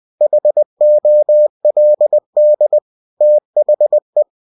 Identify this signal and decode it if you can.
Morse code, heard as a keyed tone, about 20 wpm, 600 Hz. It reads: HOLD THE
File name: cw-books / winnie-the-pooh / winnie-20wpm-page-0058